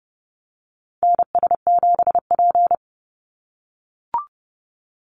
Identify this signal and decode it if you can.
Morse code, heard as a keyed tone, about 30 wpm, 700 Hz. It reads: NS7P
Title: NA SOTA Activator Callsigns 30wpm